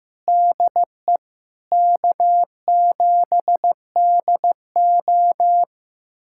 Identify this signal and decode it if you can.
Morse code, heard as a keyed tone, about 16 wpm, 700 Hz. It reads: DE K7DO
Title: USA Callsign DE Callsign - 4 Characters - 15wpm